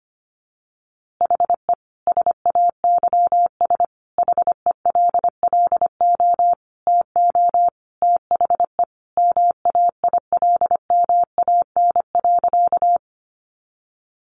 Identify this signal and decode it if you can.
Morse code, heard as a keyed tone, about 25 wpm, 700 Hz. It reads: HE SAYS HELLO TO THE MAILMAN.
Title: Easy Phrases 25wpm